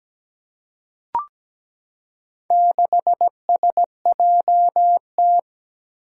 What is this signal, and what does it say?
Morse code, heard as a keyed tone, about 17 wpm, 700 Hz. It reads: 6SJT
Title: Four Challenging Letter-Number Combinations 17wpm